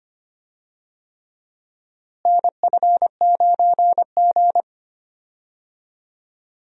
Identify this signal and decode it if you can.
Morse code, heard as a keyed tone, about 25 wpm, 700 Hz. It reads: NF9G
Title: USA Call Signs - 4 Characters - Repeated 3x - 25wpm